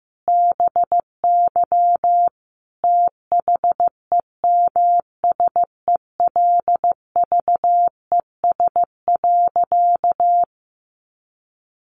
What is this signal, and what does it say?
Morse code, heard as a keyed tone, about 14 wpm, 700 Hz. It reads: BY THEMSELVES.
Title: Sentences from Top 800 Words 15wpm